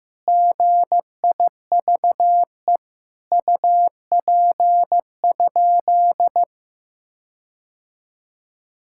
Morse code, heard as a keyed tone, about 15 wpm, 700 Hz. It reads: GIVE UP?
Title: Sentences from Top 1600 Words 15wpm